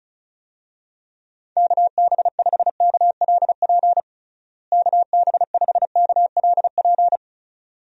Morse code, heard as a keyed tone, about 34 wpm, 700 Hz. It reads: KB5KLP KB5KLP
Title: USA Call Signs - 6 Characters - Repeated 2x - 35wpm